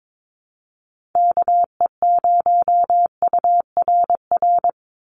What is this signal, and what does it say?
Morse code, heard as a keyed tone, about 22 wpm, 700 Hz. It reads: KE0URR
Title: USA Call Signs 22wpm